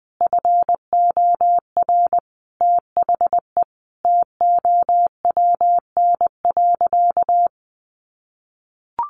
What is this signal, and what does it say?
Morse code, heard as a keyed tone, about 20 wpm, 700 Hz. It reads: FOR THE TOWN.
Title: Sentences from Top 1500 Words 20wpm